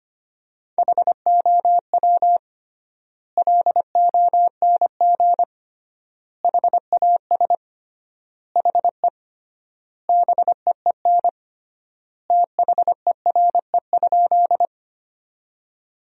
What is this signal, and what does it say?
Morse code, heard as a keyed tone, about 25 wpm, 700 Hz. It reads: HOW LONG HAS HE BEEN THERE?
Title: 3x Word-Spacing - Sentences from Top 100 Words - 25wpm